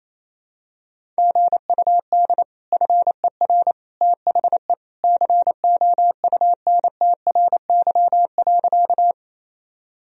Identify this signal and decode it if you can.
Morse code, heard as a keyed tone, about 28 wpm, 700 Hz. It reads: GUD FER THE COUNTRY.
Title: Sentences from Top 300 Words and Abbreviations 28wpm